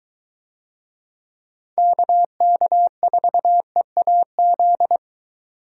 Morse code, heard as a keyed tone, about 22 wpm, 700 Hz. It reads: KK4EAZ